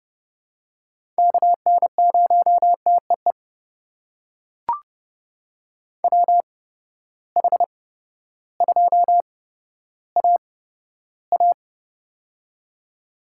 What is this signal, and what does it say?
Morse code, heard as a keyed tone, about 30 wpm, spaced as for 14 wpm, 700 Hz. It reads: KN0TEE WH2AA
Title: USA Call Signs - 8x Character Spacing - 30wpm